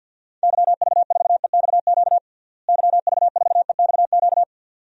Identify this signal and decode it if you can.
Morse code, heard as a keyed tone, about 50 wpm, 700 Hz. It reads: YV4EXQ YV4EXQ
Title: DXCC-100 Call Signs - 6 Characters - Repeated 2x - 50wpm